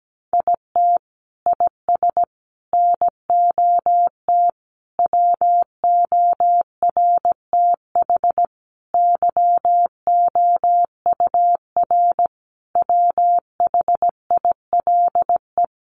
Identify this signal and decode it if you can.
Morse code, heard as a keyed tone, about 17 wpm, 700 Hz. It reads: IT IS NOT WORTH YOUR WHILE
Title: Sentences from Top 800 Words 17wpm